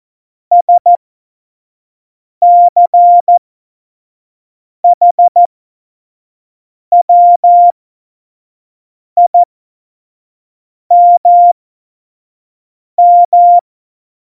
Morse code, heard as a keyed tone, about 14 wpm, spaced as for 5 wpm, 700 Hz. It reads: SCHWIMM
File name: Morsekurs / Witze / Witz 11. Dezember 2018, 25 F70